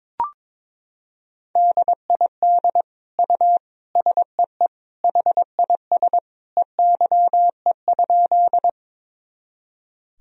Morse code, heard as a keyed tone, about 22 wpm, 700 Hz. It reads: DID U SEE HIS EYE?
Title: Sentences from Top 300 Words and Abbreviations 22wpm